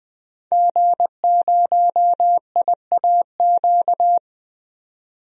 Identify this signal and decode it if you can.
Morse code, heard as a keyed tone, about 20 wpm, 700 Hz. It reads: G0IAQ
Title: DXCC-100 Call Signs - 5 Characters - Repeated 3x - 20wpm